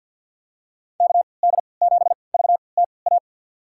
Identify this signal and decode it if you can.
Morse code, heard as a keyed tone, about 51 wpm, spaced as for 28 wpm, 700 Hz. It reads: KD7VTA
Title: USA Call Signs - 3x Character Spacing - 50wpm